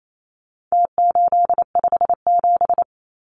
Morse code, heard as a keyed tone, about 28 wpm, 700 Hz. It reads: T857
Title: Numbers - Contest Style - 4-Digit 28wpm